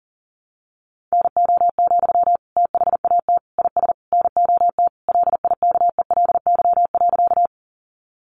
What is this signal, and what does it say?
Morse code, heard as a keyed tone, about 40 wpm, 700 Hz. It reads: NO, THAT IS NOT LIKELY.
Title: Sentences from Top 600 Words 40wpm